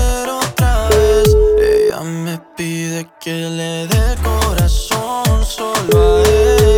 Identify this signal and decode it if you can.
Ringback tone (Europe, 425 Hz), with noise.